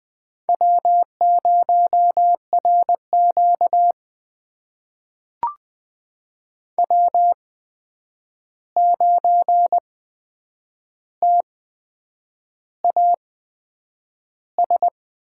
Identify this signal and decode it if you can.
Morse code, heard as a keyed tone, about 20 wpm, spaced as for 8 wpm, 700 Hz. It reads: W0RQ W9TAS
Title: USA Call Signs - 8x Character Spacing - 20wpm